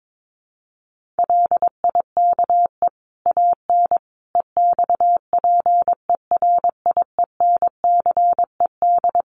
Morse code, heard as a keyed tone, about 22 wpm, 700 Hz. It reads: LIKE AN EXPERIENCED
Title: Sentences from Top 1500 Words 22wpm